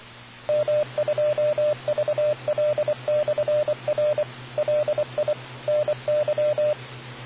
Morse code, heard as a keyed tone, about 23 wpm, 600 Hz, with radio noise. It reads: M2VL/R LI NY